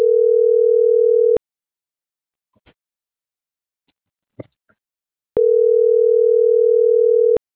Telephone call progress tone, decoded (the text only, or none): ringback tone